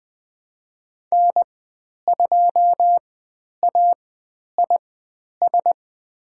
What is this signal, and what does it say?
Morse code, heard as a keyed tone, about 20 wpm, spaced as for 10 wpm, 700 Hz. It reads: N2AIS
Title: 102 Lesson 17g - Letter u - Call Signs 20wpm